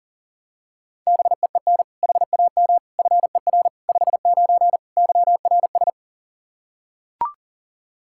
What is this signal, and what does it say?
Morse code, heard as a keyed tone, about 37 wpm, 700 Hz. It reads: BEEN HAM FER 59 YRS
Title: Practice QSO - Exchange 3 - More Info - 40wpm